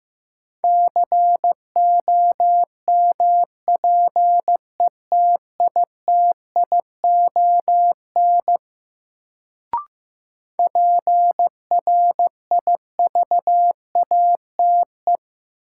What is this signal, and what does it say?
Morse code, heard as a keyed tone, about 15 wpm, 700 Hz. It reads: COMPETITION PRIVATE
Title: Sentences from Top 1000 Words - New Word Drill - 15wpm